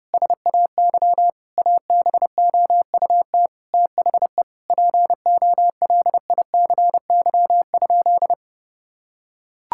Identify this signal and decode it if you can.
Morse code, heard as a keyed tone, about 30 wpm, 700 Hz. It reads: SAY ABOUT THE POLICY?
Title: Sentences from Top 1000 Words - Review Part 5 - 30wpm